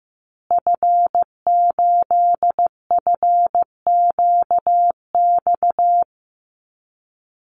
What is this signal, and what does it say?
Morse code, heard as a keyed tone, about 15 wpm, 700 Hz. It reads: F8FQX
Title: Call Signs Repeated 3x - 15wpm